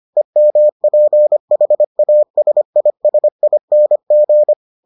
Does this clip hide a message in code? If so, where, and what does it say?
Yes: Morse code from 0.2 to 4.5 s, heard as a keyed tone, about 25 wpm, 600 Hz. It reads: EMPHASISING